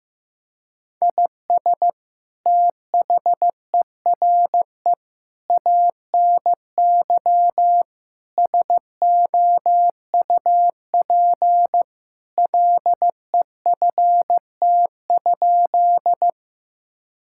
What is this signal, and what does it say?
Morse code, heard as a keyed tone, about 15 wpm, 700 Hz. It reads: IS THERE ANY SOUP LEFT?
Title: Sentences from Top 1000 Words - Review Part 6 - 15wpm